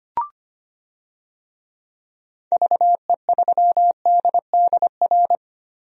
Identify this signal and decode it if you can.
Morse code, heard as a keyed tone, about 25 wpm, 700 Hz. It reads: VE3DDR
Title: DXCC-100 Call Signs - 6 Characters - Repeated 3x - 25wpm